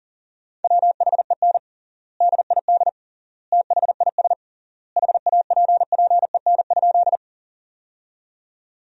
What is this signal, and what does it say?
Morse code, heard as a keyed tone, about 40 wpm, 700 Hz. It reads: WHEN DID THIS HAPPEN?
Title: 3x Word-Spacing - Sentences from Top 400 Words - 40wpm